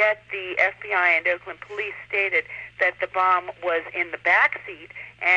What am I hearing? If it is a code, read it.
Emergency Alert System (SAME) tones, heard as harsh buzzing bursts, with speech. It is heard from 0.8 to 4.6 s.